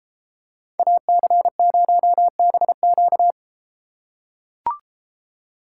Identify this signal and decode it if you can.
Morse code, heard as a keyed tone, about 33 wpm, 700 Hz. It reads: AC0BQ